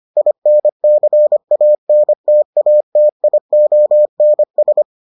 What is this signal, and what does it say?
Morse code, heard as a keyed tone, about 25 wpm, 600 Hz. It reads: INCANTATIONS